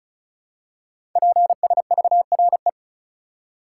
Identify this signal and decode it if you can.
Morse code, heard as a keyed tone, about 36 wpm, 700 Hz. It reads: PSVRE
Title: Code Groups - 5 Letter:Numbers - 35wpm